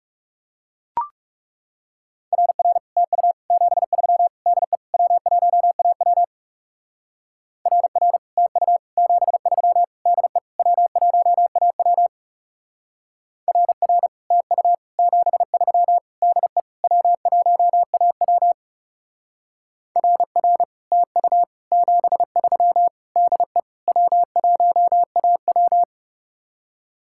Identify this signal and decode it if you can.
Morse code, heard as a keyed tone, about 36 wpm, 700 Hz. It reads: RR TU 73 DE W1AW RR TU 73 DE W1AW RR TU 73 DE W1AW RR TU 73 DE W1AW
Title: Speed-Racing Common QSO Phrases Encore - 30wpm